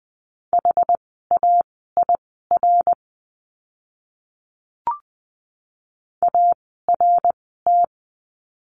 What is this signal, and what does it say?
Morse code, heard as a keyed tone, about 20 wpm, spaced as for 15 wpm, 700 Hz. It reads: HAIR ART
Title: 101 Lesson 17f - Letter u - Words 20wpm